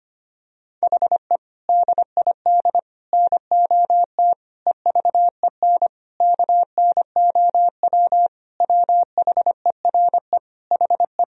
Morse code, heard as a keyed tone, about 25 wpm, 700 Hz. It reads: HE DID NOT EVEN KNOW WHERE HE
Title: Sentences from Top 1000 Words - Review Part 1 - 25wpm